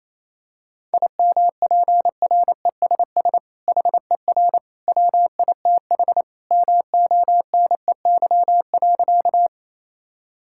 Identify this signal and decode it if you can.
Morse code, heard as a keyed tone, about 27 wpm, 700 Hz. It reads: IMPRESS HER WITH MONEY.
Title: Sentences from Top 1600 Words 28wpm